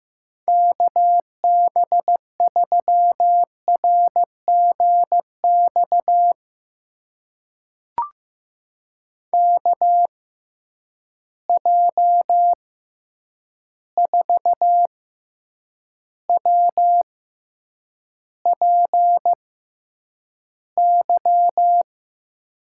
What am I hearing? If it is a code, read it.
Morse code, heard as a keyed tone, about 15 wpm, spaced as for 8 wpm, 700 Hz. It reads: KB3RGX KJ4WPY
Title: USA Call Signs - 6 Characters - 6x Character Spacing - 15wpm